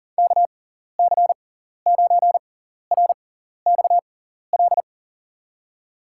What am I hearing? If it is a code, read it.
Morse code, heard as a keyed tone, about 37 wpm, spaced as for 14 wpm, 700 Hz. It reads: KC9RXL
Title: USA Call Signs - 6x Character Spacing - 40wpm